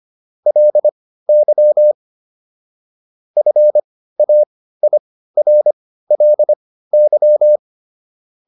Morse code, heard as a keyed tone, about 25 wpm, spaced as for 15 wpm, 600 Hz. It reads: LY FAIRLY